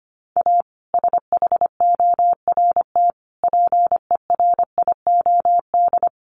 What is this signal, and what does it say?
Morse code, heard as a keyed tone, about 25 wpm, 700 Hz. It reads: A SHORT PERIOD
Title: Sets of 5 Words Encore 25wpm